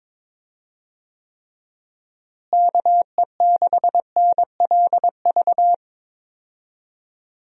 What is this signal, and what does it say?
Morse code, heard as a keyed tone, about 22 wpm, 700 Hz. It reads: KE6NLV
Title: USA Call Signs - 6 Characters - Repeated 3x - 22wpm